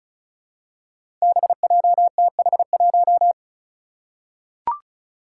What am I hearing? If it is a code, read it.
Morse code, heard as a keyed tone, about 36 wpm, 700 Hz. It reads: BJTH1
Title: Five Challenging Letter-Number Combinations 35wpm